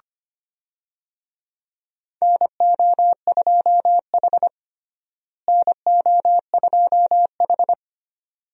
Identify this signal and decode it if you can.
Morse code, heard as a keyed tone, about 25 wpm, 700 Hz. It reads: NO2H NO2H
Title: USA Call Signs - 4 Characters - Repeated 2x - 25wpm